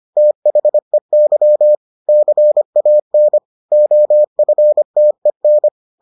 Morse code, heard as a keyed tone, about 25 wpm, 600 Hz. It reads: THEY CAN OFTEN